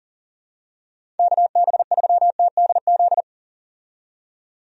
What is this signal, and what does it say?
Morse code, heard as a keyed tone, about 37 wpm, 700 Hz. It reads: KB3TDZ
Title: USA Call Signs - 6 Characters - 40wpm